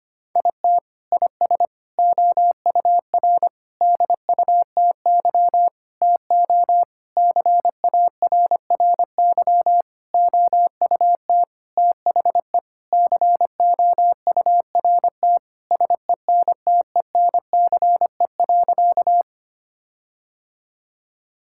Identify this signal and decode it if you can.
Morse code, heard as a keyed tone, about 25 wpm, 700 Hz. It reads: IT IS OUR DUTY TO CARRY OUT THE COURT SENTENCE.
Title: Sentences from Top 1600 Words 25wpm